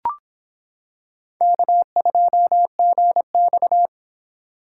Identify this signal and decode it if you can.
Morse code, heard as a keyed tone, about 26 wpm, 700 Hz. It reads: K2GX